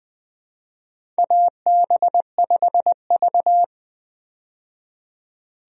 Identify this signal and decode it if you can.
Morse code, heard as a keyed tone, about 20 wpm, 700 Hz. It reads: AB5V
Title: USA Call Signs - 4 Characters - 20wpm